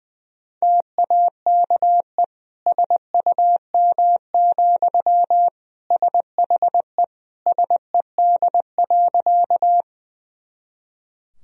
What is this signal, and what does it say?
Morse code, heard as a keyed tone, about 20 wpm, 700 Hz. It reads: TAKE SUM, SHE SED.
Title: Sentences from Top 200 Words and Abbreviations 20wpm